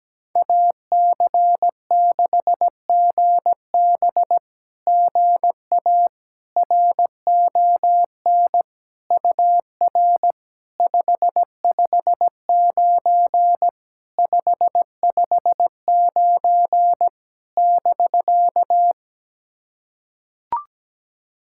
Morse code, heard as a keyed tone, about 18 wpm, 700 Hz. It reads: AC6GB GA RON UR 559 559 <BK>
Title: SOTA QSO Practice 17wpm